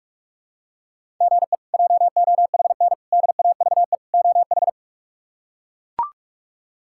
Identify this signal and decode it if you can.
Morse code, heard as a keyed tone, about 41 wpm, 700 Hz. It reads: GE JOHN DAVE OH
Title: K1USN SST QSO Practice - 45wpm